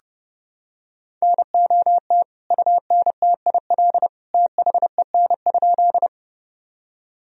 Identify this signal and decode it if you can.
Morse code, heard as a keyed tone, about 32 wpm, 700 Hz. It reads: NOT UNTIL THEN?
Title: Sentences from Top 1000 Words - Review Part 1 - 30wpm